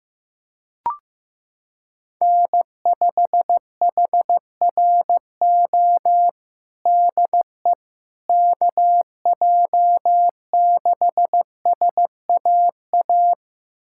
Morse code, heard as a keyed tone, about 15 wpm, 700 Hz. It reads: N5HRO DE KJ6SAA
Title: USA Callsign DE Callsign 15wpm